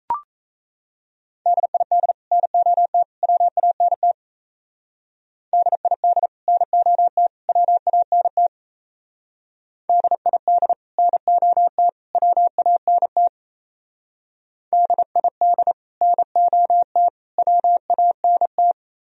Morse code, heard as a keyed tone, about 34 wpm, 700 Hz. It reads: DID NOT WANT DID NOT WANT DID NOT WANT DID NOT WANT
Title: Speed-Racing - Sets of 3 Words 28wpm